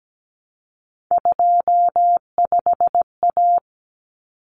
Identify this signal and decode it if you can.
Morse code, heard as a keyed tone, about 17 wpm, 700 Hz. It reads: 25A
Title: Common Cut Numbers - 3-Digit 17wpm